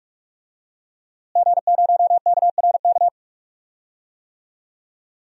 Morse code, heard as a keyed tone, about 46 wpm, 700 Hz. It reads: G0KRK
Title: DXCC-100 Call Signs - 5 Characters - Repeated 3x - 45wpm